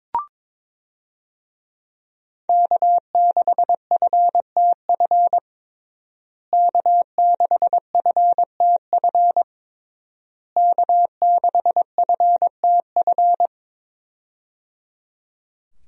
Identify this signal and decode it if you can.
Morse code, heard as a keyed tone, about 22 wpm, 700 Hz. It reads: K6FTF K6FTF K6FTF
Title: USA Call Signs - 5 Characters - Repeated 3x - 22wpm